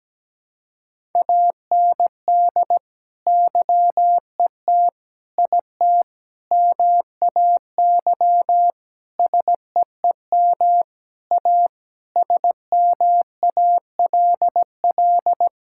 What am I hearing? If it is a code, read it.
Morse code, heard as a keyed tone, about 17 wpm, 700 Hz. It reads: AND YET IT MAY SEEM A SMALL
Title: Sentences from Top 1000 Words - Review Part 5 - 17wpm